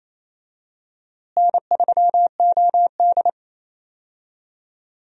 Morse code, heard as a keyed tone, about 28 wpm, 700 Hz. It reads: N3OD
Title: USA Call Signs - 4 Characters - Repeated 2x - 28wpm